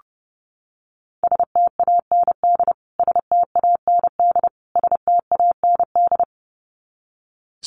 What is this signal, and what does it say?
Morse code, heard as a keyed tone, about 30 wpm, 700 Hz. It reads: STAND STAND STAND
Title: Top 300 Words Repeated 30wpm